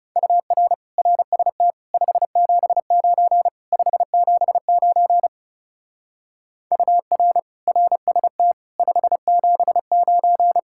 Morse code, heard as a keyed tone, about 36 wpm, 700 Hz. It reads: UR RST 579 579 UR RST 579
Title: Speed-Racing Common QSO Phrases - 30wpm